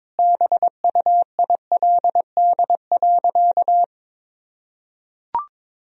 Morse code, heard as a keyed tone, about 22 wpm, 700 Hz. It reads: BUILD.
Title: Sentences from Top 1000 Words - Review Part 1 - 22wpm